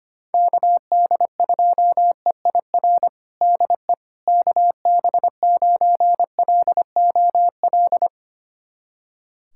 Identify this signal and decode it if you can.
Morse code, heard as a keyed tone, about 25 wpm, 700 Hz. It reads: KD2EIR DE KB9LOL
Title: USA Callsign DE Callsign - 6 Characters - 25wpm